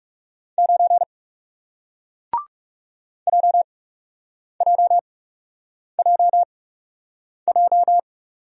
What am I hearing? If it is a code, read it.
Morse code, heard as a keyed tone, about 40 wpm, spaced as for 8 wpm, 700 Hz. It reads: 9 JJJJ